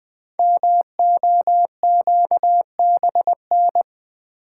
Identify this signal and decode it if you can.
Morse code, heard as a keyed tone, about 20 wpm, 700 Hz. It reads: MOQBN
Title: Code Groups - 5 Letters - 20wpm